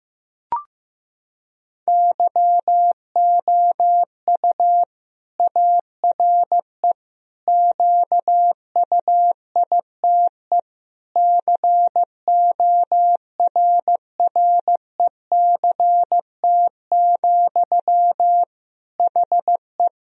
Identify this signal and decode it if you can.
Morse code, heard as a keyed tone, about 15 wpm, 700 Hz. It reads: YOU ARE QUITE CORRECT, HE
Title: Sentences from Top 1000 Words - Review Part 2 - 15wpm